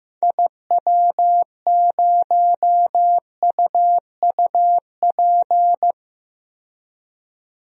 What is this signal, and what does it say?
Morse code, heard as a keyed tone, about 15 wpm, 700 Hz. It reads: IW0UUP
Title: DXCC-100 Call Signs - 6 Characters - 15wpm